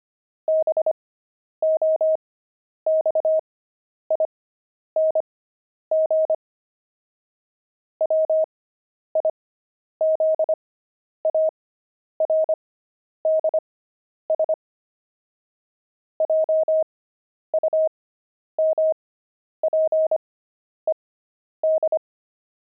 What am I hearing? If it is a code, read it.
Morse code, heard as a keyed tone, about 25 wpm, spaced as for 10 wpm, 630 Hz. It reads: BOXING WIZARDS JUMPED
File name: pangram-fs-10-cs-25-jumping